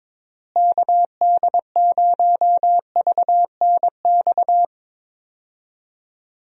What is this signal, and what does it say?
Morse code, heard as a keyed tone, about 22 wpm, 700 Hz. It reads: KD0VNX
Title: USA Call Signs - 6 Characters - Repeated 2x - 22wpm